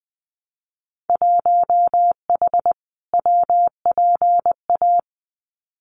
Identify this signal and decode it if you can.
Morse code, heard as a keyed tone, about 20 wpm, 700 Hz. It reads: 1H WPA
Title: Winter Field Day Exchanges 20wpm